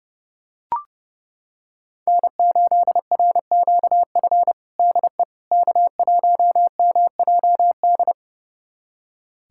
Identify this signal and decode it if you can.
Morse code, heard as a keyed tone, about 30 wpm, 700 Hz. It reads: N8RQF DE K1MJD
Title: USA Callsign DE Callsign - 5 Characters - 30wpm